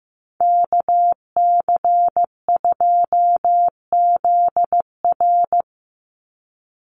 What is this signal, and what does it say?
Morse code, heard as a keyed tone, about 15 wpm, 700 Hz. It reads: KC2ZR
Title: USA Call Signs 15wpm